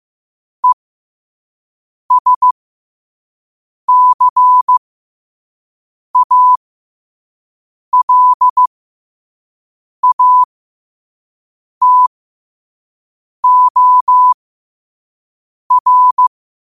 Morse code, heard as a keyed tone, about 15 wpm, spaced as for 5 wpm, 1000 Hz. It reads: ESCALATOR
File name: qotdfiles / 5wpm